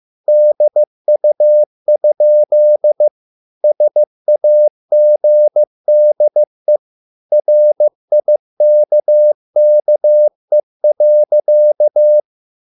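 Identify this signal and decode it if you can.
Morse code, heard as a keyed tone, about 15 wpm, 600 Hz. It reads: DU? SAGDE RIKKE.
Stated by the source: Morse code at a slow 15 wpm from CW med Gnister.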